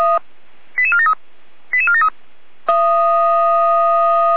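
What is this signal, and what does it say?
Five-tone selective call (ZVEI), heard as quick runs of five tones, tones of about 70 ms. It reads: 88473 88473
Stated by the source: Five-tone ZVEI selective call, received on SDR.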